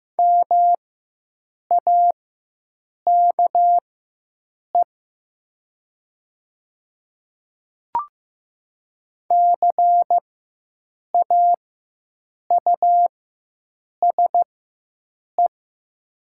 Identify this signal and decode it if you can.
Morse code, heard as a keyed tone, about 15 wpm, spaced as for 7 wpm, 700 Hz. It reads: MAKE CAUSE
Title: 4x Character Spacing - Top 200 Words - 15wpm